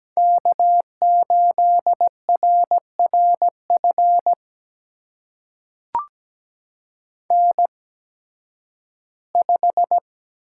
Morse code, heard as a keyed tone, about 17 wpm, spaced as for 11 wpm, 700 Hz. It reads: K8RRF N5
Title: USA Call Signs - 5 Characters - 8x Character Spacing - 17wpm